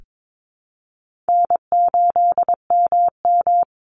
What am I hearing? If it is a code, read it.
Morse code, heard as a keyed tone, about 22 wpm, 700 Hz. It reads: N8MM
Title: USA Call Signs 22wpm